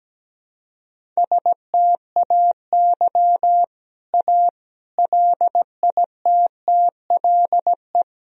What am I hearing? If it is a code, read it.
Morse code, heard as a keyed tone, about 17 wpm, 700 Hz. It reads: STAY A LITTLE
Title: Sentences from Top 1000 Words - Review Part 1 - 17wpm